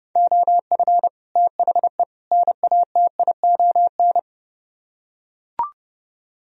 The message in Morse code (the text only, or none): OF THE NATION